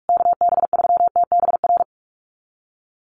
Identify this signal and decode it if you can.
Morse code, heard as a keyed tone, about 46 wpm, 690 Hz. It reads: KB3TBR
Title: USA Call Signs Repeated 3x 45wpm